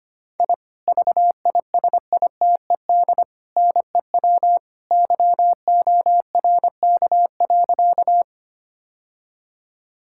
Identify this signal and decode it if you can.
Morse code, heard as a keyed tone, about 25 wpm, 700 Hz. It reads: I VISITED NEW YORK.
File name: practice / 25wpm / Sentences with 2-Word Phrases - 25wpm